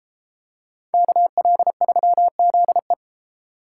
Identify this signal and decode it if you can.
Morse code, heard as a keyed tone, about 31 wpm, 700 Hz. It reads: KL3ZE